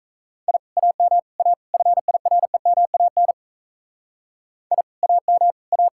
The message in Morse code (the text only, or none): I AM A FIREMAN I AM A